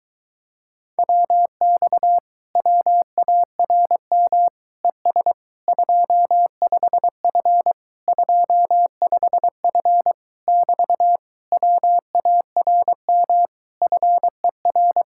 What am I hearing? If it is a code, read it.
Morse code, heard as a keyed tone, about 22 wpm, 700 Hz. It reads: WX WARM ES 25F 25F <BT> WARM FER